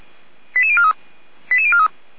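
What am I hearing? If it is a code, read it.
Five-tone selective call (ZVEI), heard as quick runs of five tones, tones of about 70 ms. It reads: 88043 88043
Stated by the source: ZVEI five-tone fire brigade alert received on SDR, Feuerwehr Orlamünde.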